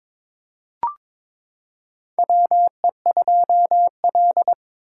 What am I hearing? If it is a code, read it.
Morse code, heard as a keyed tone, about 22 wpm, 700 Hz. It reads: WE2L
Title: USA Callsign DE Callsign - 4 Characters - 22wpm